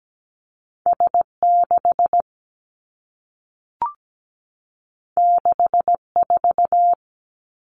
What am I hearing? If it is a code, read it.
Morse code, heard as a keyed tone, about 17 wpm, 700 Hz. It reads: S 6 6 4
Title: Two Challenging Letter-Number Combinations 17wpm